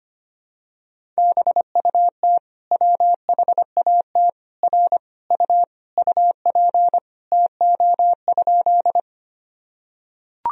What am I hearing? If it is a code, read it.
Morse code, heard as a keyed tone, about 25 wpm, 700 Hz. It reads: BUT WHAT R U UP TO?
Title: Sentences from Top 100 Words and Abbreviations 25wpm